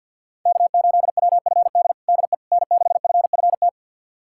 Morse code, heard as a keyed tone, about 46 wpm, 700 Hz. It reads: K8WFD DE N6FFT
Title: USA Callsign DE Callsign - 5 Characters - 50wpm